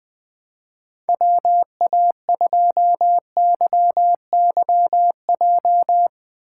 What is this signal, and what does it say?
Morse code, heard as a keyed tone, about 20 wpm, 700 Hz. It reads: WA2YYJ
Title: USA Call Signs - 6x Character Spacing - 20wpm